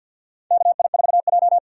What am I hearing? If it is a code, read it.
Morse code, heard as a keyed tone, about 47 wpm, 700 Hz. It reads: KI4J
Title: USA Call Signs - 4 Characters - 6x Character Spacing - 50wpm